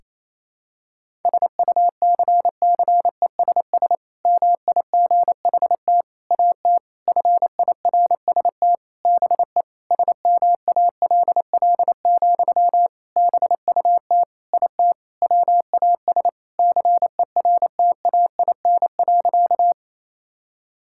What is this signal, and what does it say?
Morse code, heard as a keyed tone, about 28 wpm, 700 Hz. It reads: SUCCESS MIGHT AT FIRST BE SMALL, BUT IT WAS CERTAIN.
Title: Sentences from Top 1000 Words - Review Part 6 - 28wpm